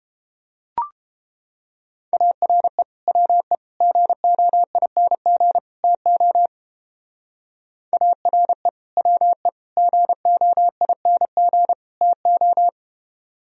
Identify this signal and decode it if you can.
Morse code, heard as a keyed tone, about 29 wpm, 700 Hz. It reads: ARE WE GOING TO ARE WE GOING TO